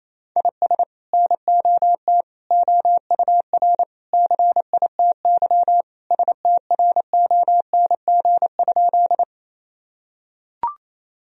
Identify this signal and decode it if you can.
Morse code, heard as a keyed tone, about 28 wpm, 700 Hz. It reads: IS NOT OUR CITY STRONG?
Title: Sentences from Top 1000 Words - Review Part 3 - 28wpm